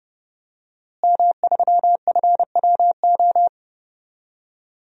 Morse code, heard as a keyed tone, about 32 wpm, 700 Hz. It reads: M3FWO
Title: DXCC-100 Call Signs - 5 Characters - Repeated 3x - 30wpm